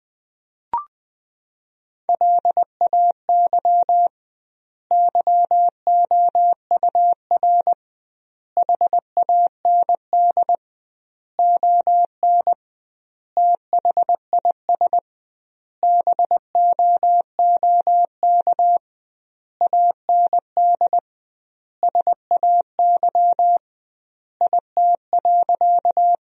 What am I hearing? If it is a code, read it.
Morse code, heard as a keyed tone, about 20 wpm, 700 Hz. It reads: LAY YOUR HAND ON THIS BOOK AND SAY IT.
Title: 2x Word-Spacing - Sentences from Top 400 Words - 20wpm